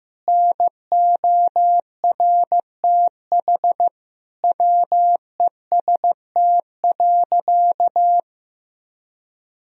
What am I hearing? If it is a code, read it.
Morse code, heard as a keyed tone, about 15 wpm, 700 Hz. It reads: NORTH WEST.
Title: Sentences from Top 1000 Words - Review Part 2 - 15wpm